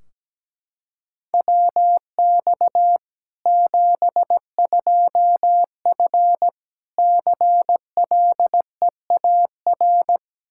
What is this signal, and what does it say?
Morse code, heard as a keyed tone, about 17 wpm, 700 Hz. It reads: WX 72F CLEAR
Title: Practice QSO - Exchange 2 - Info - 17wpm